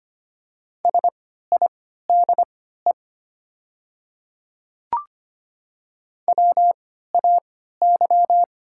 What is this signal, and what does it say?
Morse code, heard as a keyed tone, about 25 wpm, spaced as for 14 wpm, 700 Hz. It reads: SIDE WAY
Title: 3x Character Spacing - Top 100 Words - 25wpm